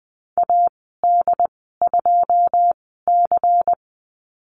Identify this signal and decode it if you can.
Morse code, heard as a keyed tone, about 20 wpm, spaced as for 15 wpm, 700 Hz. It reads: AD2C
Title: 128 Lesson 21g - Number 6 - Call Signs 20wpm